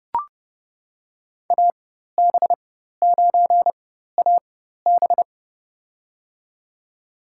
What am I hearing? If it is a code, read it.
Morse code, heard as a keyed tone, about 30 wpm, spaced as for 14 wpm, 700 Hz. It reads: AB9AB
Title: USA Call Signs - 5 Characters - 4x Character Spacing - 30wpm